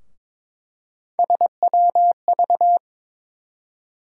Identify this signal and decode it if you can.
Morse code, heard as a keyed tone, about 22 wpm, 700 Hz. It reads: SWV
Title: USA Call Sign Postfixes - 3 Letters 22wpm